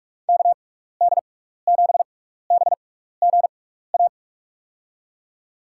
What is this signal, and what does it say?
Morse code, heard as a keyed tone, about 44 wpm, spaced as for 15 wpm, 700 Hz. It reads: KD7BGA
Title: USA Call Signs - 6 Characters - 6x Character Spacing - 45wpm